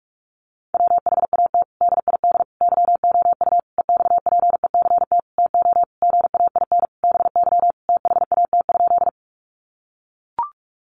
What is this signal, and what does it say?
Morse code, heard as a keyed tone, about 44 wpm, 700 Hz. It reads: WHAT DID YOU EXPECT TO GAIN BY THAT?
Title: Sentences from Top 1100 Words 45wpm